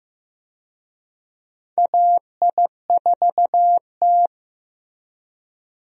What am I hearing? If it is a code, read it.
Morse code, heard as a keyed tone, about 15 wpm, 700 Hz. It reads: AI4T
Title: USA Call Signs - 4 Characters - Repeated 2x - 15wpm